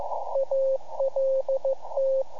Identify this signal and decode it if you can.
Morse code, heard as a keyed tone, about 15 wpm, 540 Hz, with radio noise. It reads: ALT
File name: iHAB-3 / Repeater Recordings